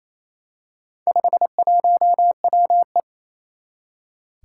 Morse code, heard as a keyed tone, about 28 wpm, 700 Hz. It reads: 51WE
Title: Four Challenging Letter-Number Combinations 28wpm